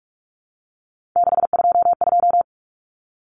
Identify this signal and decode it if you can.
Morse code, heard as a keyed tone, about 44 wpm, 700 Hz. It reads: 622
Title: Common Cut Numbers - 3-Digit 45wpm